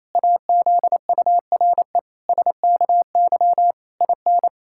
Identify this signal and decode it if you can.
Morse code, heard as a keyed tone, about 28 wpm, 700 Hz. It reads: AZURE SKY IN
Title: Pangram Challenge v2 - 28wpm